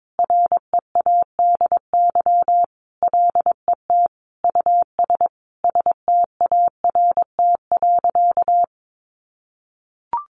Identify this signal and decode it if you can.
Morse code, heard as a keyed tone, about 22 wpm, 700 Hz. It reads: READY LET US START.
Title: Sentences from Top 700 Words 22wpm